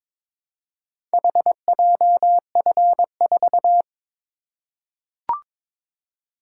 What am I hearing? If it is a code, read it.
Morse code, heard as a keyed tone, about 22 wpm, 700 Hz. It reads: HJF4
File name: practice / 22wpm / Four Letter-Number Combinations - 22wpm